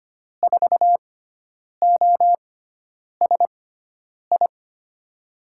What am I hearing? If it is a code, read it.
Morse code, heard as a keyed tone, about 25 wpm, spaced as for 9 wpm, 700 Hz. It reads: 4OSI
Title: USA Call Signs - 6 Characters - 6x Character Spacing - 25wpm